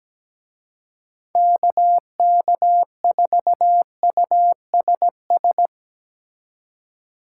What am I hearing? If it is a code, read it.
Morse code, heard as a keyed tone, about 18 wpm, 700 Hz. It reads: KK4USS
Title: Speed Racing - USA 2x3 Call Signs 17wpm